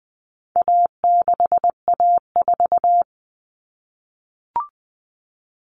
Morse code, heard as a keyed tone, about 20 wpm, 700 Hz. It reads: A6A4
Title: All Cut Numbers - 4-Digit 20wpm